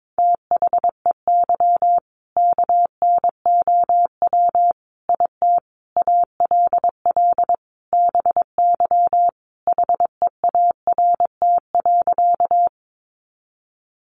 Morse code, heard as a keyed tone, about 22 wpm, 700 Hz. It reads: THEY KNOW IT ALL BY HEART.
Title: Sentences from Top 1200 Words 22wpm